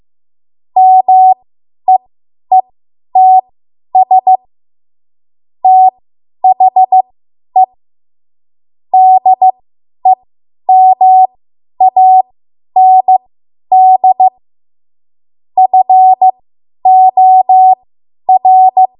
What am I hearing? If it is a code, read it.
Morse code, heard as a keyed tone, about 15 wpm, spaced as for 10 wpm, 750 Hz. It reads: MEETS THE DEMAND FOR
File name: Archive / 10 WPM / 10WPM